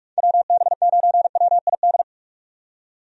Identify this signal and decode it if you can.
Morse code, heard as a keyed tone, about 46 wpm, 700 Hz. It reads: WB9WID
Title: USA Call Signs - 6 Characters - 45wpm